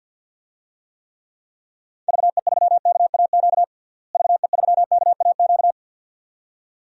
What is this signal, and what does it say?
Morse code, heard as a keyed tone, about 52 wpm, 700 Hz. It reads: VE3KAQ VE3KAQ
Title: DXCC-100 Call Signs - 6 Characters - Repeated 2x - 50wpm